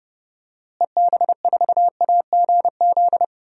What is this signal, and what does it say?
Morse code, heard as a keyed tone, about 30 wpm, 700 Hz. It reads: EB4AGZ
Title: DXCC-100 Call Signs - 6 Characters - Repeated 3x - 30wpm